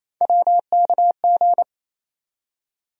Morse code, heard as a keyed tone, about 27 wpm, 700 Hz. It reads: WKG
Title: USA Call Sign Postfixes - 3 Letters 28wpm